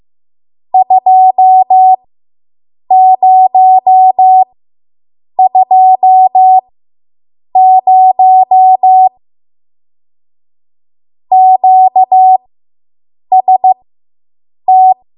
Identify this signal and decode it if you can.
Morse code, heard as a keyed tone, about 15 wpm, spaced as for 7 wpm, 750 Hz. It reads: 2020 QST